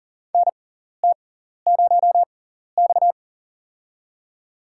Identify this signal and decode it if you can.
Morse code, heard as a keyed tone, about 40 wpm, spaced as for 14 wpm, 700 Hz. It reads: NT0X
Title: USA Call Signs - 4 Characters - 6x Character Spacing - 40wpm